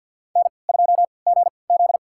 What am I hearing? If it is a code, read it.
Morse code, heard as a keyed tone, about 47 wpm, spaced as for 28 wpm, 700 Hz. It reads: N2GZ
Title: USA Call Signs - 4 Characters - 3x Character Spacing - 50wpm